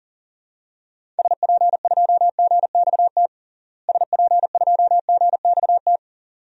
Morse code, heard as a keyed tone, about 40 wpm, 700 Hz. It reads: SP2GXT SP2GXT
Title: DXCC-100 Call Signs - 6 Characters - Repeated 2x - 40wpm